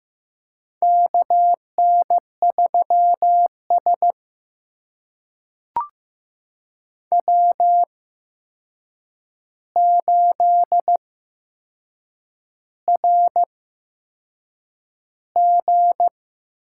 Morse code, heard as a keyed tone, about 15 wpm, spaced as for 6 wpm, 700 Hz. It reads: KN3S W8RG
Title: USA Call Signs - 4 Characters - 8x Character Spacing - 15wpm